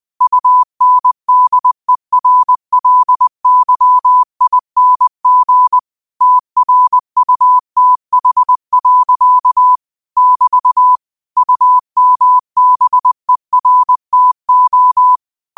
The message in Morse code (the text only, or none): UNDERLYING TRUTH. <BT> UMBERTO